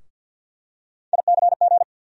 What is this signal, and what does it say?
Morse code, heard as a keyed tone, about 47 wpm, 700 Hz. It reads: ICG
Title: USA Call Sign Postfixes - 3 Letters 50wpm